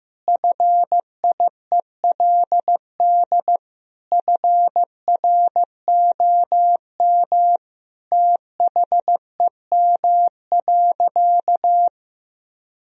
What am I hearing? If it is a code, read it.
Morse code, heard as a keyed tone, about 15 wpm, 700 Hz. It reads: FIELD FROM THEM.
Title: Sentences from Top 1000 Words - Review Part 2 - 15wpm